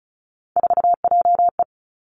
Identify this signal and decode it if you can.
Morse code, heard as a keyed tone, about 36 wpm, 700 Hz. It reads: 4JE
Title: Three Challenging Letter-Number Combinations 35wpm